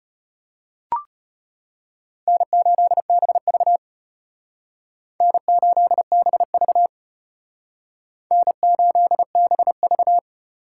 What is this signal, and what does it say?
Morse code, heard as a keyed tone, about 32 wpm, 700 Hz. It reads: N8BV N8BV N8BV